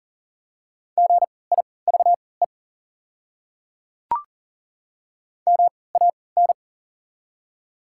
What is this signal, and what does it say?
Morse code, heard as a keyed tone, about 37 wpm, spaced as for 22 wpm, 690 Hz. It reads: GIVE MAN
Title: 3x Character Spacing - Top 200 Words - 40wpm